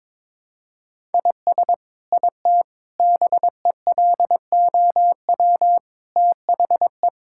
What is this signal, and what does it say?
Morse code, heard as a keyed tone, about 22 wpm, 700 Hz. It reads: IS IT BELOW THE
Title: Sentences from Top 1000 Words - Review Part 5 - 22wpm